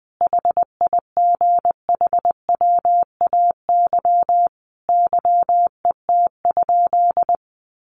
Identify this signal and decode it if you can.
Morse code, heard as a keyed tone, about 20 wpm, 700 Hz. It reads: HIGHWAY YET?
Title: Sentences from Top 1400 Words 20wpm